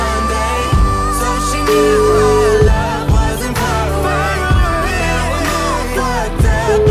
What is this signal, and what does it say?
Ringback tone (Europe, 425 Hz), with noise.